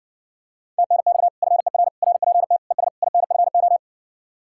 Morse code, heard as a keyed tone, about 60 wpm, 700 Hz. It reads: TNX FER RPT ES INFO